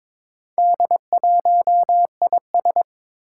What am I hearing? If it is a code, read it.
Morse code, heard as a keyed tone, about 22 wpm, 700 Hz. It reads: D1IS
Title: Four Challenging Letter-Number Combinations 22wpm